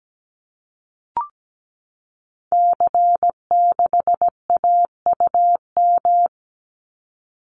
Morse code, heard as a keyed tone, about 17 wpm, 700 Hz. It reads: C6AUM
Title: Call Signs Repeated 3x - 17wpm